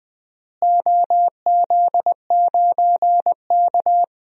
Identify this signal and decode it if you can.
Morse code, heard as a keyed tone, about 20 wpm, 700 Hz. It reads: OZ9K
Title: DXCC-100 Call Signs - 4 Characters - Repeated 3x - 20wpm